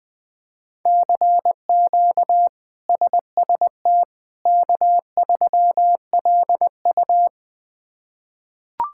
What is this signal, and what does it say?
Morse code, heard as a keyed tone, about 20 wpm, 700 Hz. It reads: CQ SST K3LU
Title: K1USN SST QSO Practice - 20wpm